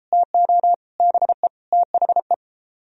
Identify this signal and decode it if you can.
Morse code, heard as a keyed tone, about 30 wpm, 700 Hz. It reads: TO BE THE